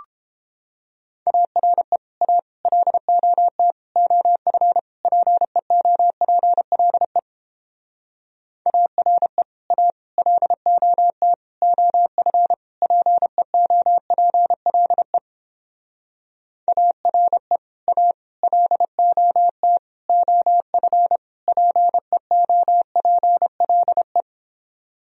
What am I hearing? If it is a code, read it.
Morse code, heard as a keyed tone, about 29 wpm, 700 Hz. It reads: ARE A LOT OF PEOPLE ARE A LOT OF PEOPLE ARE A LOT OF PEOPLE